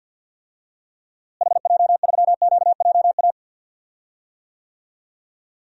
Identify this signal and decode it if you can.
Morse code, heard as a keyed tone, about 52 wpm, 700 Hz. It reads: HJ3QJA
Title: DXCC-100 Call Signs - 6 Characters - Repeated 2x - 50wpm